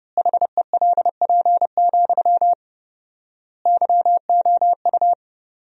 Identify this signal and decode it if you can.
Morse code, heard as a keyed tone, about 30 wpm, 700 Hz. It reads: HELP, YOU
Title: 4x Word-Spacing - Sentences from Top 300 Words - 30wpm